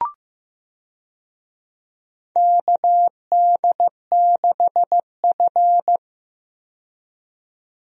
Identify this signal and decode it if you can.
Morse code, heard as a keyed tone, about 15 wpm, 700 Hz. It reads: KD6F
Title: USA Call Signs - 4 Characters - 15wpm